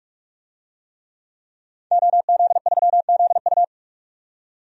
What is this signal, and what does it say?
Morse code, heard as a keyed tone, about 44 wpm, 700 Hz. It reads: OZ3ZU
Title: DXCC-100 Call Signs - 5 Characters - 45wpm